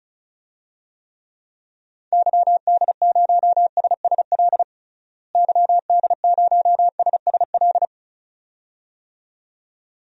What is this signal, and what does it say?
Morse code, heard as a keyed tone, about 36 wpm, 700 Hz. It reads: YD0SSL YD0SSL
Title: DXCC-100 Call Signs - 6 Characters - Repeated 2x - 35wpm